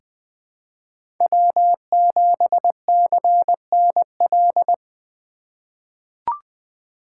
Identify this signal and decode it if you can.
Morse code, heard as a keyed tone, about 21 wpm, 700 Hz. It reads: W7CNL
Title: 2023 SOTA Chaser Call Signs - 20wpm